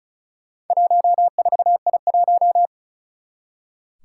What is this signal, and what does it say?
Morse code, heard as a keyed tone, about 36 wpm, 700 Hz. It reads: 14I1
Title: Four Challenging Letter-Number Combinations 35wpm